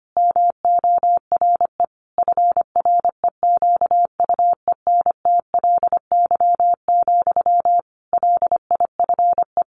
Morse code, heard as a keyed tone, about 25 wpm, 700 Hz. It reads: MORE FREQUENTLY, LIFE
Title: Sentences from Top 1300 Words 25wpm